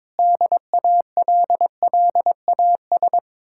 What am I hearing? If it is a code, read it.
Morse code, heard as a keyed tone, about 22 wpm, 700 Hz. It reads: DALLAS
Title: Speed-Racing Common QSO Phrases Encore - 22wpm